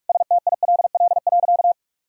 Morse code, heard as a keyed tone, about 46 wpm, 700 Hz. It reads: STILL.
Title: Sentences from Top 500 Words and Abbreviations 45wpm